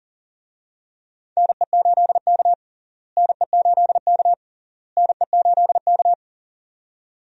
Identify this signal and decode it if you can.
Morse code, heard as a keyed tone, about 40 wpm, 700 Hz. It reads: NE8K NE8K NE8K
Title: USA Call Signs - 4 Characters - Repeated 3x - 40wpm